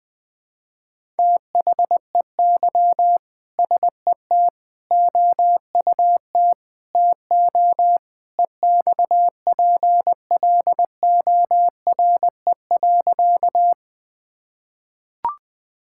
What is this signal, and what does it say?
Morse code, heard as a keyed tone, about 20 wpm, 700 Hz. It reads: THEY SET OUT TO EXPLORE.
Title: Sentences with 2-Word Phrases - 20wpm